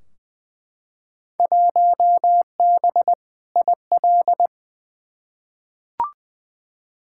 Morse code, heard as a keyed tone, about 20 wpm, 700 Hz. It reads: 1B IL
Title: Field Day Exchanges Encore - 20wpm